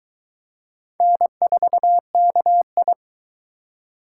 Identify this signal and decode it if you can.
Morse code, heard as a keyed tone, about 22 wpm, 700 Hz. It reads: N4KI